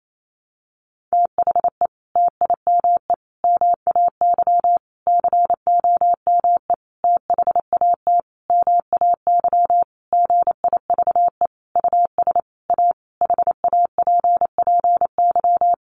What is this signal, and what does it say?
Morse code, heard as a keyed tone, about 28 wpm, 700 Hz. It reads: THE TIME MAY COME THAT MAY GIVE US A HAPPY
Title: Sentences from Top 1100 Words 28wpm